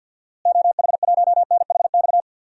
Morse code, heard as a keyed tone, about 53 wpm, 700 Hz. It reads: OH1NHX
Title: DXCC-100 Call Signs - 6 Characters - Repeated 3x - 50wpm